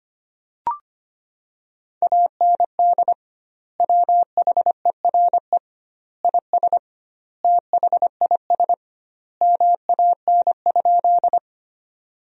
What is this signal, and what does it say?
Morse code, heard as a keyed tone, about 25 wpm, 700 Hz. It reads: AND WHERE IS THIS MAN?
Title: 2x Word-Spacing - Sentences from Top 200 Words - 25wpm